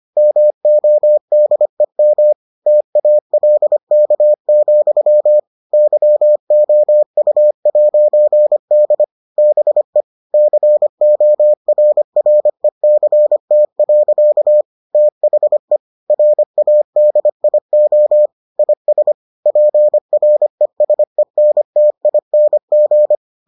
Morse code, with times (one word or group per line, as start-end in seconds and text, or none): 0.2-2.3 MODEM
2.7-5.4 TALK,
5.7-9.0 YOU'D
9.4-10.0 BE
10.3-14.6 CORRECT.
14.9-15.8 THE
16.1-18.3 RADIO
18.6-19.1 IS
19.5-23.1 PRESENTING